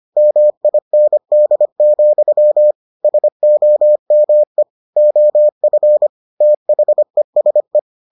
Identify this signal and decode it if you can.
Morse code, heard as a keyed tone, about 25 wpm, 600 Hz. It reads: MIND, SOME OF THESE